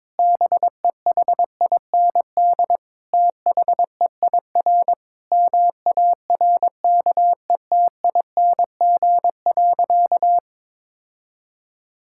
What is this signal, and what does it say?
Morse code, heard as a keyed tone, about 22 wpm, 700 Hz. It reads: BEHIND THEIR MARKETING.
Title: Sentences from Top 1000 Words - Review Part 5 - 22wpm